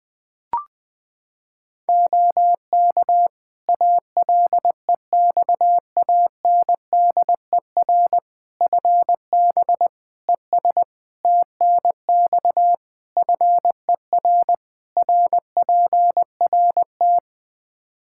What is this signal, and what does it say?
Morse code, heard as a keyed tone, about 21 wpm, 700 Hz. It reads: OK ALEXANDER FB ES TNX FER RPRT
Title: Practice QSO - Exchange 2 - Info - 20wpm